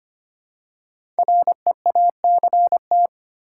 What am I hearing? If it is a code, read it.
Morse code, heard as a keyed tone, about 25 wpm, 700 Hz. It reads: REACT
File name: 5 Letter Words - First 3 Letters Given - 25wpm